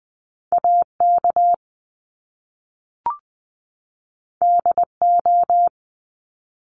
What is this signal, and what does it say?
Morse code, heard as a keyed tone, about 20 wpm, 700 Hz. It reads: A K D O
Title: Top Two Letter Combinations in Words 20wpm